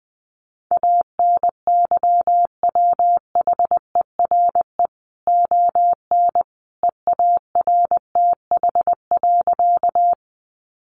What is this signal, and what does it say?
Morse code, heard as a keyed tone, about 20 wpm, 700 Hz. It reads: ANYWHERE ON EARTH.
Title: Sentences from Top 1400 Words 20wpm